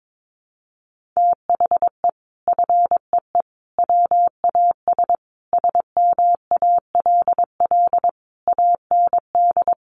Morse code, heard as a keyed tone, about 22 wpm, 700 Hz. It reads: THE FEE WAS SMALL AND
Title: Sentences from Top 1500 Words 22wpm